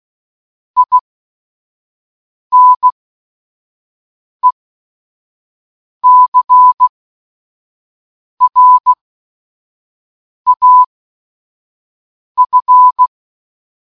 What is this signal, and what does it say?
Morse code, heard as a keyed tone, about 16 wpm, spaced as for 5 wpm, 1000 Hz. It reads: INECRAF
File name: escaperoom-19-morsecode-minecraft